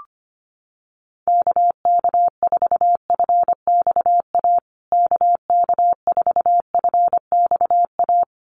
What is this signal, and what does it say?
Morse code, heard as a keyed tone, about 25 wpm, 700 Hz. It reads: KK4FXA KK4FXA
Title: USA Call Signs Repeated 2x 25wpm